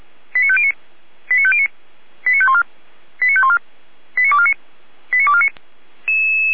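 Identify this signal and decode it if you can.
Five-tone selective call (ZVEI), heard as quick runs of five tones, tones of about 70 ms. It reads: 89509 89509 89514 89514 89249 89249
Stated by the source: ZVEI five-tone call 89163 received by SDR.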